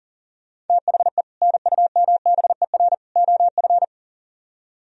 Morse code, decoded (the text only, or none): THE NUMBER OF